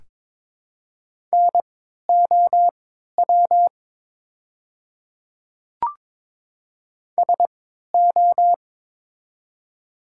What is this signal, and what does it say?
Morse code, heard as a keyed tone, about 23 wpm, spaced as for 13 wpm, 700 Hz. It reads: NOW SO
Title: 3x Character Spacing - Top 100 Words - 22wpm